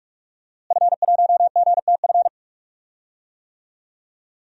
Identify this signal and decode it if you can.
Morse code, heard as a keyed tone, about 47 wpm, 700 Hz. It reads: F1GTF
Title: DXCC-100 Call Signs - 5 Characters - Repeated 2x - 45wpm